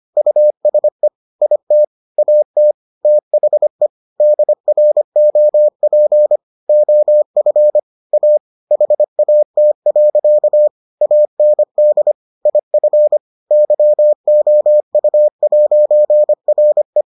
Morse code, heard as a keyed tone, about 25 wpm, 600 Hz. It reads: USE IT AT THE DROP OF A HAT. AND IF YOU'RE